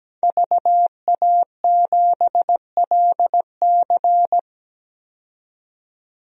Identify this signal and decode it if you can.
Morse code, heard as a keyed tone, about 17 wpm, 700 Hz. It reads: VA7LC
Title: DXCC-100 Call Signs - 5 Characters - 17wpm